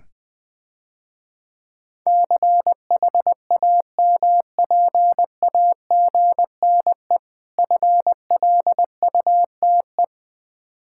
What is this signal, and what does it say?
Morse code, heard as a keyed tone, about 20 wpm, 700 Hz. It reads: CHAMPAGNE FLUTE
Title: Difficult Words - Speed-Racing with Hints - 20wpm